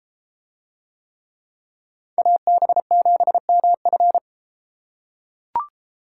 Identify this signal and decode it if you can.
Morse code, heard as a keyed tone, about 32 wpm, 700 Hz. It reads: AB7MF